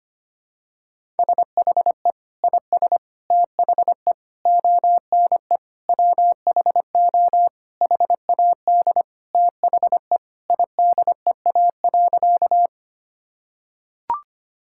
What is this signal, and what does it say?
Morse code, heard as a keyed tone, about 25 wpm, 700 Hz. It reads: SHE IS THE ONE WHO HAD THE IDEA.
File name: practice / 25wpm / Sentences with 2-Word Phrases - 25wpm